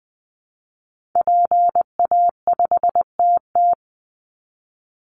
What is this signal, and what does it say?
Morse code, heard as a keyed tone, about 20 wpm, 700 Hz. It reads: PA5TT
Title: Call Signs Repeated 2x - 20wpm